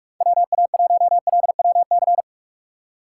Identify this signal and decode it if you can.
Morse code, heard as a keyed tone, about 46 wpm, 700 Hz. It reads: WA1LWC
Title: K1USN SST QSO Practice - 45wpm